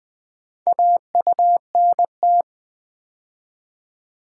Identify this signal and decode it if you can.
Morse code, heard as a keyed tone, about 20 wpm, 700 Hz. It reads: AUNT
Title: Build A Word - 2 Letters At A Time - 4-Letter Words - 20wpm